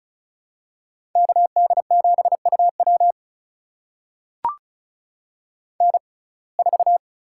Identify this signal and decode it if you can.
Morse code, heard as a keyed tone, about 36 wpm, spaced as for 25 wpm, 700 Hz. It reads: KD7UW N4
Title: USA Call Signs - 5 Characters - 6x Character Spacing - 35wpm